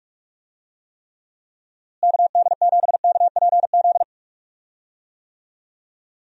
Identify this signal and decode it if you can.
Morse code, heard as a keyed tone, about 46 wpm, 700 Hz. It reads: KD7KPZ
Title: USA Call Signs - 6 Characters - Repeated 2x - 45wpm